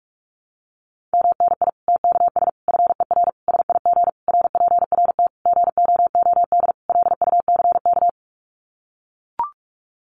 Morse code, heard as a keyed tone, about 44 wpm, 700 Hz. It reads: MNI TKS FER SIG RPRT GOOD LUCK
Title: Common QSO Phrases 45wpm